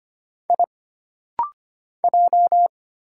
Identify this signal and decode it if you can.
Morse code, heard as a keyed tone, about 25 wpm, 700 Hz. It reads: I J
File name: Single Character (Letter:Number:Punctuation:Prosign) - Rapid-Fire - 25wpm